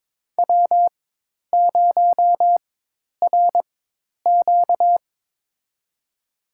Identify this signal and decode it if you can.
Morse code, heard as a keyed tone, about 22 wpm, spaced as for 10 wpm, 700 Hz. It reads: W0RQ
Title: USA Call Signs - 4 Characters - 4x Character Spacing - 22wpm